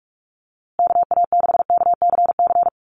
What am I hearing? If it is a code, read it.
Morse code, heard as a keyed tone, about 44 wpm, 700 Hz. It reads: KA6KCC
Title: USA Call Signs Repeated 2x 45wpm